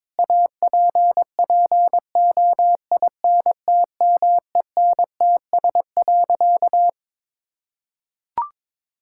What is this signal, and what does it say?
Morse code, heard as a keyed tone, about 22 wpm, 700 Hz. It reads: APPOINTMENTS.
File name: Difficult Sentences with Hints - 22wpm